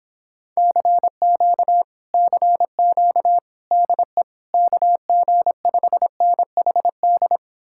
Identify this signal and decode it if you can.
Morse code, heard as a keyed tone, about 29 wpm, 700 Hz. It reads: CQ CQ DE KG5NHD